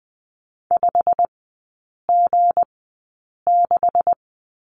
Morse code, heard as a keyed tone, about 20 wpm, spaced as for 8 wpm, 700 Hz. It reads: 5G6
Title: 159 Lesson 26e - Letter g - Three Characters 20wpm